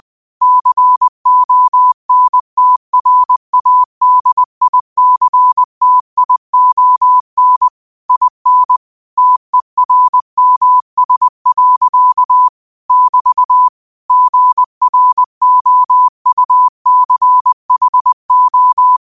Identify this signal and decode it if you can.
Morse code, heard as a keyed tone, about 20 wpm, 1000 Hz. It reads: CONTRADICTION IN TERMS. <BT> GROUCHO
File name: qotdfiles / 20wpm